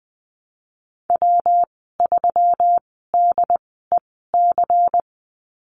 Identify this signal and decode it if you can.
Morse code, heard as a keyed tone, about 20 wpm, spaced as for 15 wpm, 700 Hz. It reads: W3DEC
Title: 122 Lesson 20g - Number 3 - Call Signs 20wpm